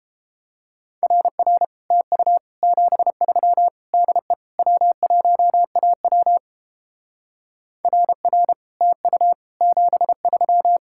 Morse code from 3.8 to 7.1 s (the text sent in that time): DE W1AW